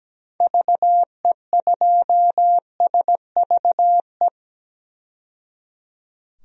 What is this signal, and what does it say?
Morse code, heard as a keyed tone, about 17 wpm, 700 Hz. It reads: VE2SVE
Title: DXCC-100 Call Signs - 6 Characters - 17wpm